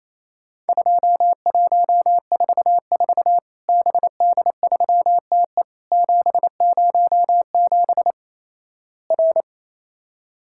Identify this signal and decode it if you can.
Morse code, heard as a keyed tone, about 28 wpm, 700 Hz. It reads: 2144 BD3TE 707 R
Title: Worked All Europe DX Contest - QTC Exchange - 28wpm